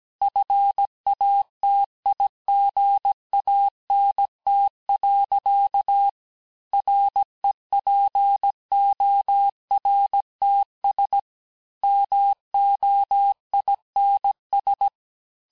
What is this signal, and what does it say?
Morse code, heard as a keyed tone, about 17 wpm, 800 Hz. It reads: FATIGANT. REPORTS MOINS
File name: corrige-cours-1581-fin-du-QSO-vitesse-16-mpm